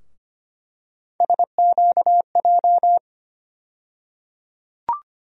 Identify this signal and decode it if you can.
Morse code, heard as a keyed tone, about 25 wpm, 700 Hz. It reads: SQJ
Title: USA Call Sign Postfixes - 3 Letters 25wpm